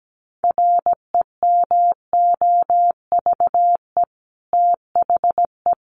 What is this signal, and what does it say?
Morse code, heard as a keyed tone, about 17 wpm, 700 Hz. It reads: REMOVE THE
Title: Sentences from Top 1300 Words 17wpm